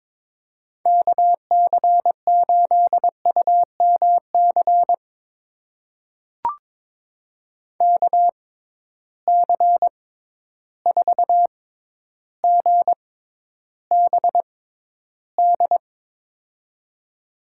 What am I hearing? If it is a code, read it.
Morse code, heard as a keyed tone, about 22 wpm, spaced as for 11 wpm, 700 Hz. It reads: KC8UMC KC4GBD
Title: USA Call Signs - 6 Characters - 6x Character Spacing - 22wpm